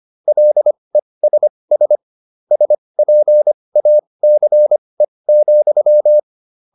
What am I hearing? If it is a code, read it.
Morse code, heard as a keyed tone, about 25 wpm, spaced as for 20 wpm, 600 Hz. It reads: LESS SPACE,